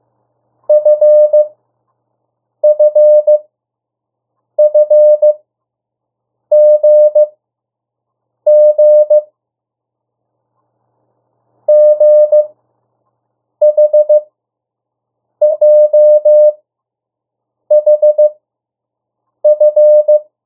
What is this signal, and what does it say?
Morse code, heard as a keyed tone, about 15 wpm, spaced as for 6 wpm, 600 Hz. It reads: FFFGG GHJHF
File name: morse mp3 / 02 urok 1